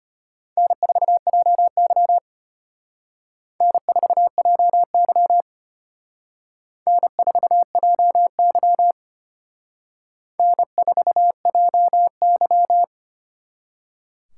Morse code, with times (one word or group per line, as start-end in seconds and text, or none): 0.6-2.2 N4JY
3.6-5.4 N4JY
6.9-8.9 N4JY
10.4-12.8 N4JY